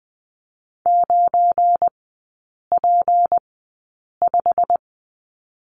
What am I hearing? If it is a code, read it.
Morse code, heard as a keyed tone, about 20 wpm, spaced as for 8 wpm, 690 Hz. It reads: 9P5
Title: 212 Lesson 34e - Number 8 - Three Characters 20wpm